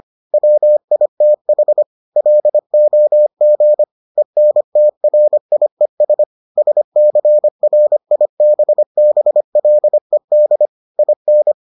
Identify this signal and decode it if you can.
Morse code, heard as a keyed tone, about 25 wpm, 600 Hz. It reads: WITH LOG ENTRIES SCRIBBLED IN